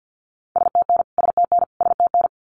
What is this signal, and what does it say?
Morse code, heard as a keyed tone, about 50 wpm, 700 Hz. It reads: STN STN STN
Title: Abbreviations Repeated 3x - 50wpm